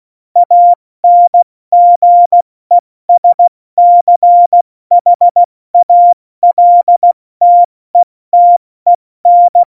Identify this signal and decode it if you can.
Morse code, heard as a keyed tone, about 16 wpm, 700 Hz. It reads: ANGESCHALTETEN
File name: Morsekurs / Witze / Witz 08. Januar 2019, 70 F80